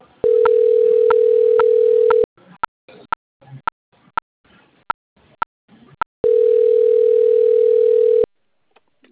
Ringback tone (North America), 440 + 480 Hz.